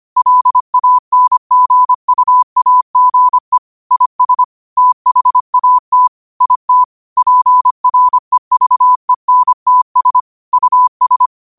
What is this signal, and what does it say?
Morse code, heard as a keyed tone, about 25 wpm, 1000 Hz. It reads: LANGUAGE IS THAT IT PREVENTS US
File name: qotdfiles / 25wpm